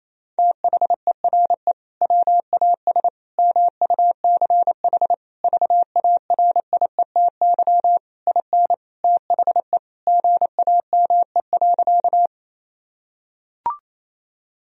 Morse code, heard as a keyed tone, about 28 wpm, 700 Hz. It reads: THERE WAS MUCH VARIETY IN THE GAME.
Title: Sentences from Top 1000 Words - Review Part 4 - 28wpm